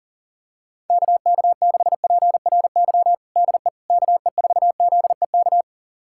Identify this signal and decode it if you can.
Morse code, heard as a keyed tone, about 40 wpm, 700 Hz. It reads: KK6PRY DE KE4ZEK
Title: USA Callsign DE Callsign - 6 Characters - 40wpm